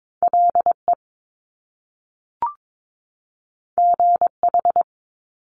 Morse code, heard as a keyed tone, about 22 wpm, 700 Hz. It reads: LE GH
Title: Top Two Letter Combinations in Words 22wpm